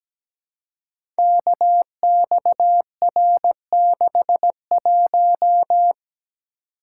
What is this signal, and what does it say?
Morse code, heard as a keyed tone, about 17 wpm, 700 Hz. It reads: KXR61
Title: Code Groups - 5 Letter:Numbers - 17wpm